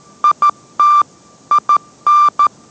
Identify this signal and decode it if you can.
Morse code, heard as a keyed tone, about 13 wpm, 1180 Hz. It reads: IT IN